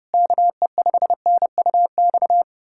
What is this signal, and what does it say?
Morse code, heard as a keyed tone, about 32 wpm, 700 Hz. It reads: KE5NUX
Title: USA Callsign DE Callsign 30wpm